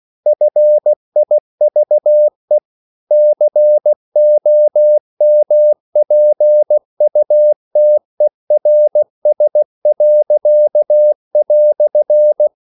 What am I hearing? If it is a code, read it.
Morse code, heard as a keyed tone, about 16 wpm, 600 Hz. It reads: FIVE COMPUTERS."
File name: CW / Practice / EnglishQuotations0416